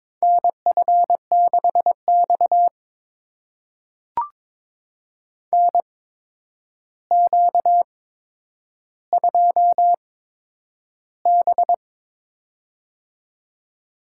Morse code, heard as a keyed tone, about 22 wpm, spaced as for 9 wpm, 700 Hz. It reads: NF6X NQ2B
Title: USA Call Signs - 4 Characters - 8x Character Spacing - 22wpm